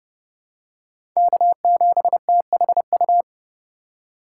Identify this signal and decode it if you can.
Morse code, heard as a keyed tone, about 30 wpm, 700 Hz. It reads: K7THU
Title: USA Call Signs - 5 Characters - 4x Character Spacing - 30wpm